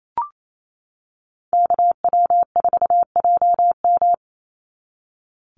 Morse code, heard as a keyed tone, about 28 wpm, 700 Hz. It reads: KW4JM
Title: NA SOTA Chaser Callsigns 28wpm